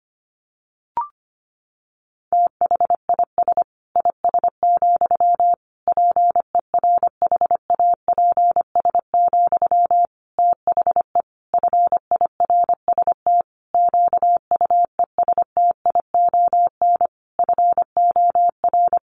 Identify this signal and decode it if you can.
Morse code, heard as a keyed tone, about 25 wpm, 700 Hz. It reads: THIS IS, PERHAPS, THE FIRST QUESTION FOR
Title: Sentences from Top 1500 Words 25wpm